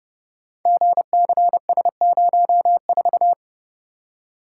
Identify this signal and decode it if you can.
Morse code, heard as a keyed tone, about 30 wpm, 700 Hz. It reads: GCS04
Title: Code Groups - 5 Letter:Numbers - 30wpm